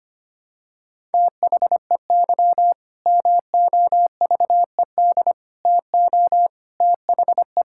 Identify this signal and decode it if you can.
Morse code, heard as a keyed tone, about 25 wpm, 700 Hz. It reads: THEY MOVED TO THE
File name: practice / 25wpm / Sentences with 4-Word Phrases - 25wpm